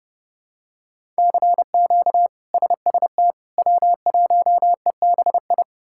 Morse code, heard as a keyed tone, about 32 wpm, 700 Hz. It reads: CQ SST W1EBI
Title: K1USN SST QSO Practice - 30wpm